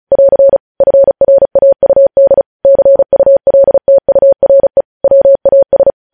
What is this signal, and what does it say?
Morse code, heard as a keyed tone, about 36 wpm, 550 Hz. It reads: <AR> FRAUD CULTURE WAS
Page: News Headlines in Morse Code at 35 WPM